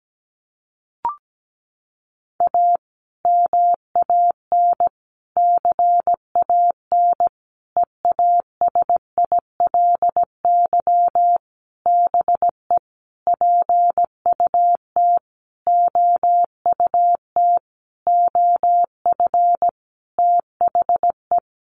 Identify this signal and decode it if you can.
Morse code, heard as a keyed tone, about 17 wpm, 700 Hz. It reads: A MAN CAN EASILY BE PUT OUT OF THE
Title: Sentences from Top 900 Words 17wpm